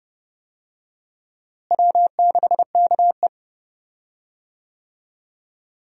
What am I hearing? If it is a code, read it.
Morse code, heard as a keyed tone, about 30 wpm, 700 Hz. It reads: W6KE
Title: USA Call Signs - 4 Characters - Repeated 3x - 30wpm